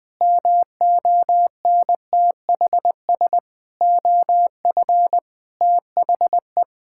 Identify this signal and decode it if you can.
Morse code, heard as a keyed tone, about 20 wpm, 700 Hz. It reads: MONTHS OF THE
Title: Sets of 8 Words - 20wpm